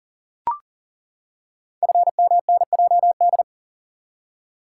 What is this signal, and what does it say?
Morse code, heard as a keyed tone, about 39 wpm, 700 Hz. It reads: FMNJD
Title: Code Groups - 5 Letters - 40wpm